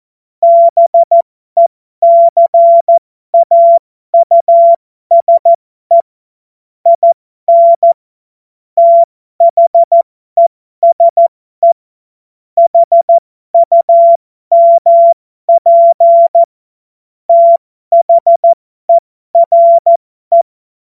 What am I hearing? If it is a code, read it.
Morse code, heard as a keyed tone, about 14 wpm, spaced as for 12 wpm, 680 Hz. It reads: BECAUSE IN THESE HUMP THERE
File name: Morsekurs / Witze